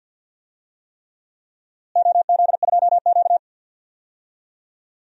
Morse code, heard as a keyed tone, about 52 wpm, 700 Hz. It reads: OZ2Q
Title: DXCC-100 Call Signs - 4 Characters - Repeated 3x - 50wpm